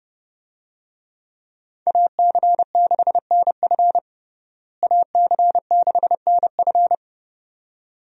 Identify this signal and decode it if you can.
Morse code, heard as a keyed tone, about 30 wpm, 700 Hz. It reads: AC6NF AC6NF
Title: USA Call Signs - 5 Characters - Repeated 2x - 30wpm